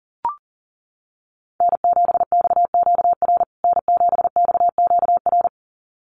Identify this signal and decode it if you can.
Morse code, heard as a keyed tone, about 40 wpm, 700 Hz. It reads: N7XQR N7XQR
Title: USA Call Signs Repeated 2x 40wpm